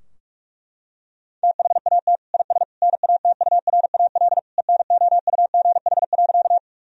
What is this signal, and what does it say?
Morse code, heard as a keyed tone, about 41 wpm, 700 Hz. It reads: THAT IS NATURAL ENOUGH.
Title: Sentences from Top 1000 Words - Review Part 4 - 45wpm